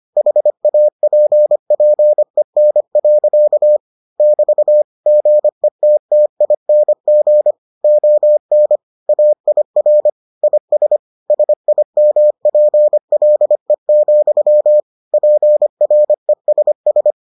Morse code, heard as a keyed tone, about 25 wpm, 600 Hz. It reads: HAPPEN. <BT> GETTING ON AIR IS SIMPLE, PRESS